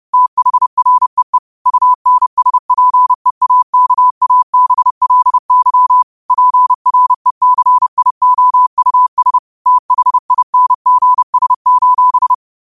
Morse code, heard as a keyed tone, about 30 wpm, 1000 Hz. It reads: THREE UNSPEAKABLY PRECIOUS THINGS:
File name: qotdfiles / 30wpm